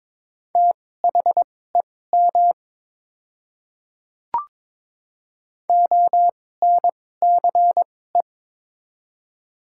Morse code, heard as a keyed tone, about 22 wpm, spaced as for 16 wpm, 700 Hz. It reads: THEM ONCE
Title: 2x Character Spacing - Top 500 Words - 22wpm